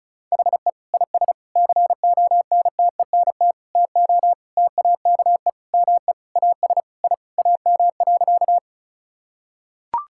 Morse code, heard as a keyed tone, about 36 wpm, 700 Hz. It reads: HE IS CONTENT TO TAKE ME AS I AM.
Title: Sentences from Top 1000 Words - Review Part 6 - 35wpm